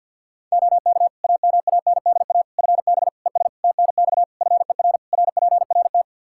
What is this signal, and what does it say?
Morse code, heard as a keyed tone, about 49 wpm, 700 Hz. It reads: OK AMANDA FB ES TNX FER RPRT
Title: K1USN SST QSO Practice 50wpm